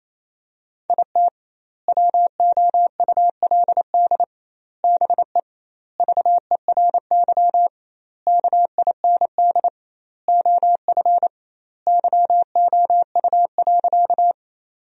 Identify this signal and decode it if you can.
Morse code, heard as a keyed tone, about 27 wpm, 700 Hz. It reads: IT WOULD BE VERY KIND OF YOU.
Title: 2x Word-Spacing - Sentences from Top 200 Words - 28wpm